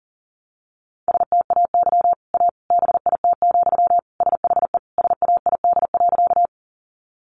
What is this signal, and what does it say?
Morse code, heard as a keyed tone, about 40 wpm, 700 Hz. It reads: STAY A BIT, SHE SAID.
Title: Sentences from Top 900 Words 40wpm